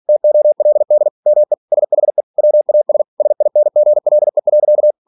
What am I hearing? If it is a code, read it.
Morse code, heard as a keyed tone, about 46 wpm, 600 Hz. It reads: TOLD ME SHE WAS SINGLE.
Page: Never 47wpm